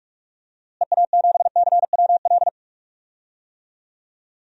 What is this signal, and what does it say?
Morse code, heard as a keyed tone, about 46 wpm, 700 Hz. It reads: EA7CWL
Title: DXCC-100 Call Signs - 6 Characters - Repeated 3x - 45wpm